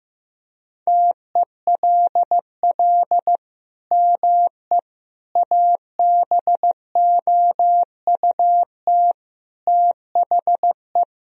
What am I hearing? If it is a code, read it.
Morse code, heard as a keyed tone, about 15 wpm, 700 Hz. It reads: TELL ME ABOUT THE
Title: Sentences from Top 1000 Words - Review Part 2 - 15wpm